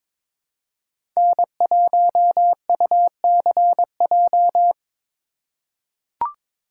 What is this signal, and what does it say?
Morse code, heard as a keyed tone, about 21 wpm, 700 Hz. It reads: N1UCJ
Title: USA Call Signs - 5 Characters - 2x Character Spacing - 22wpm